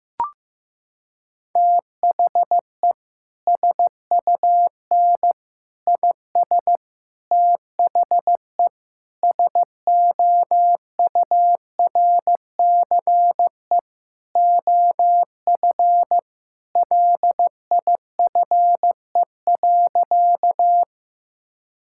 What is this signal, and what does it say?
Morse code, heard as a keyed tone, about 15 wpm, 700 Hz. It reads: THE SUN IS THE SOURCE OF LIFE.
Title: Sentences from Top 1000 Words - Review Part 5 - 15wpm